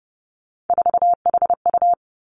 Morse code, heard as a keyed tone, about 30 wpm, 700 Hz. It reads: 4HU
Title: Three Challenging Letter-Number Combinations 30wpm